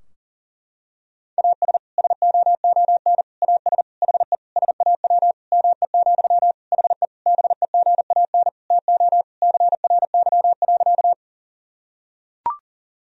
Morse code, heard as a keyed tone, about 40 wpm, 700 Hz. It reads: AS SOON AS HE SAW ME, HE BEGAN TO CRY.
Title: Sentences from Top 1000 Words - Review Part 2 - 40wpm